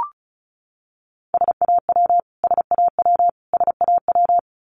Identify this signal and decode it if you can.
Morse code, heard as a keyed tone, about 34 wpm, 700 Hz. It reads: SAW SAW SAW
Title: Top 500 Words Repeated 35wpm